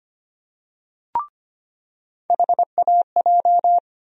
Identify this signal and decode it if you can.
Morse code, heard as a keyed tone, about 25 wpm, 700 Hz. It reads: HAJ
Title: USA Call Sign Postfixes - 3 Letters 25wpm